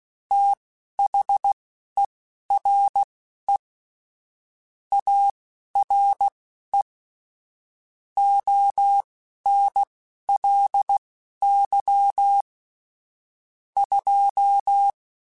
Morse code, heard as a keyed tone, about 16 wpm, spaced as for 11 wpm, 800 Hz. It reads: THERE ARE ONLY 2
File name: Technical Subjects / Technical Subjects-mp3s / AND Gate @10